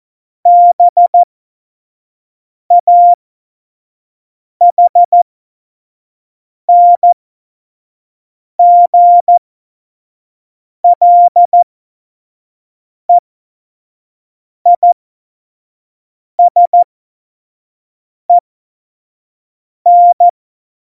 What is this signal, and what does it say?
Morse code, heard as a keyed tone, about 14 wpm, spaced as for 5 wpm, 700 Hz. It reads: BAHNGLEISEN
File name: Morsekurs / Witze / Witz 13. November 2018, 25 F70